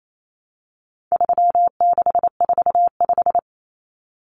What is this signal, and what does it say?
Morse code, heard as a keyed tone, about 28 wpm, 700 Hz. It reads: 3645
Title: Common Cut Numbers - 4-Digit 28wpm